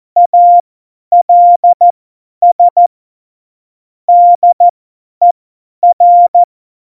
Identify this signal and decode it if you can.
Morse code, heard as a keyed tone, about 14 wpm, spaced as for 10 wpm, 700 Hz. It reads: ALS DER
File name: Morsekurs / Witze / Witz 18 August 2020, 50 F70